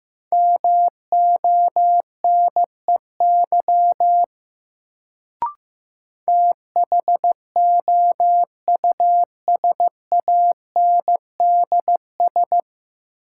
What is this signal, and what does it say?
Morse code, heard as a keyed tone, about 15 wpm, 700 Hz. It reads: MONEY THOUSANDS
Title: Sets of 4 Words - New Word Drill - 15wpm